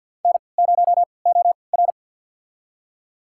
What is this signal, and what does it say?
Morse code, heard as a keyed tone, about 47 wpm, spaced as for 27 wpm, 700 Hz. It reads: N0OR
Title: USA Call Signs - 3x Character Spacing - 50wpm